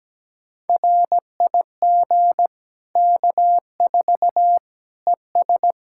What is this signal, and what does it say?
Morse code, heard as a keyed tone, about 17 wpm, 700 Hz. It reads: RIG K4 ES
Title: Practice QSO - Exchange 2 - Info - 17wpm